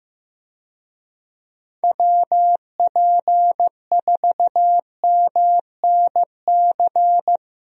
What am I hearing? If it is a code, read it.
Morse code, heard as a keyed tone, about 15 wpm, 700 Hz. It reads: WP4MNC
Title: USA Call Signs - 6 Characters - Repeated 2x - 15wpm